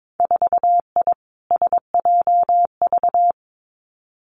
Morse code, heard as a keyed tone, about 22 wpm, 700 Hz. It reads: 4I SJV
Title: Winter Field Day Exchanges Encore 22wpm